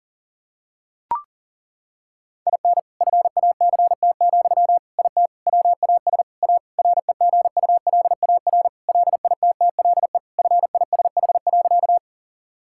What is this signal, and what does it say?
Morse code, heard as a keyed tone, about 40 wpm, 700 Hz. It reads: IN FACT, IT WAS A REGULAR LITTLE FISH.
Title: Sentences from Top 1000 Words - Review Part 6 - 40wpm